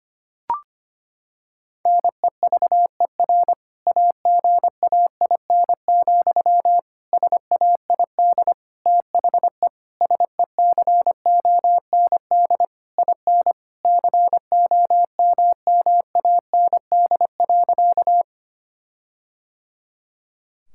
Morse code, heard as a keyed tone, about 25 wpm, 700 Hz. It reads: NEVER AGAIN, SAID THE SECOND IN COMMAND.
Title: Sentences from Top 1600 Words 25wpm